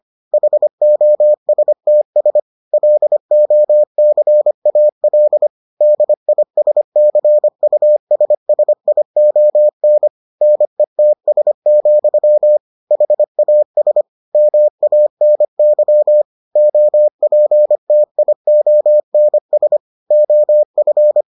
Morse code, heard as a keyed tone, about 25 wpm, 600 Hz. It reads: HOSTS LOCAL DISCUSSION NETS, HAS MANY OPTIONS OF